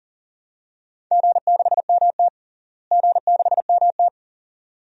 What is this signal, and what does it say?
Morse code, heard as a keyed tone, about 37 wpm, 700 Hz. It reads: G6MT G6MT
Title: DXCC-100 Call Signs - 4 Characters - Repeated 2x - 40wpm